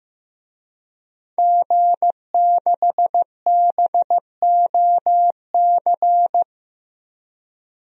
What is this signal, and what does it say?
Morse code, heard as a keyed tone, about 15 wpm, 700 Hz. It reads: G6BOC
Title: DXCC-100 Call Signs - 5 Characters - Repeated 3x - 15wpm